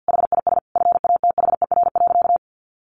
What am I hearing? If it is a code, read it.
Morse code, heard as a keyed tone, about 50 wpm, 700 Hz. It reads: HIS FATHER.
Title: Sentences from Top 500 Words 50wpm